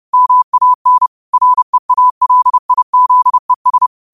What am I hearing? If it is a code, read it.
Morse code, heard as a keyed tone, about 30 wpm, 1000 Hz. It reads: MAN REALIZES
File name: qotdfiles / 30wpm